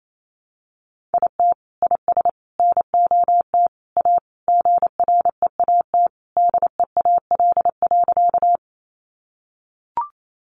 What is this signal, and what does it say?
Morse code, heard as a keyed tone, about 28 wpm, 700 Hz. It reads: IT IS NOT A GREAT DEAL.
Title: Sentences from Top 800 Words 28wpm